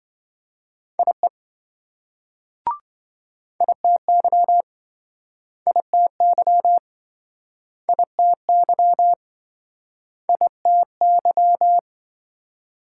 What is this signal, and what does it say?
Morse code, heard as a keyed tone, about 27 wpm, 700 Hz. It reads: IE ITY ITY ITY ITY